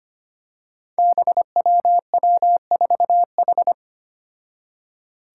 Morse code, heard as a keyed tone, about 25 wpm, 700 Hz. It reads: BWW4H
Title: Five Challenging Letter-Number Combinations 25wpm